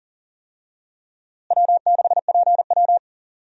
Morse code, heard as a keyed tone, about 40 wpm, 700 Hz. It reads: W6PW
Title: USA Call Signs - 4 Characters - Repeated 2x - 40wpm